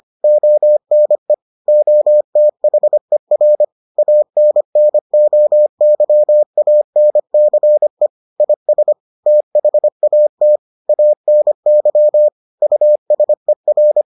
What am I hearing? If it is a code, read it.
Morse code, heard as a keyed tone, about 25 wpm, 600 Hz. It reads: ONE OTHER ANNOYANCE IS THAT ANY USER